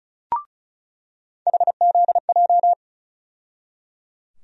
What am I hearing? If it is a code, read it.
Morse code, heard as a keyed tone, about 34 wpm, 700 Hz. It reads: HZJ
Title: USA Call Sign Postfixes - 3 Letters 35wpm